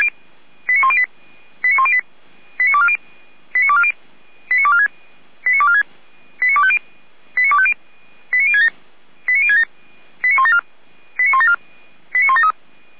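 Five-tone selective call (ZVEI), heard as quick runs of five tones, tones of about 70 ms. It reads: 89108 89108 89240 89240 89246 89246 89250 89250 89067 89067 89174 89174 89173
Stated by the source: ZVEI five-tone selective call received on SDR, from the feuerwehr-orlamuende fire brigade audio server.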